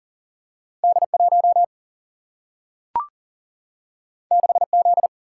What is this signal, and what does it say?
Morse code, heard as a keyed tone, about 39 wpm, 700 Hz. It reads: D1 6Z
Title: Single-Letter Single-Number 40wpm